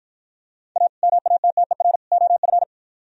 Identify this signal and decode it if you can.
Morse code, heard as a keyed tone, about 49 wpm, 700 Hz. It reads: A MATTER OF